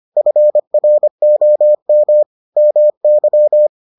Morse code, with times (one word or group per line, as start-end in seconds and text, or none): 0.2-2.2 FROM
2.6-3.7 MY